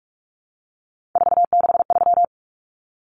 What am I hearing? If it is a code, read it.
Morse code, heard as a keyed tone, about 44 wpm, 700 Hz. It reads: 463
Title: Common Cut Numbers - 3-Digit 45wpm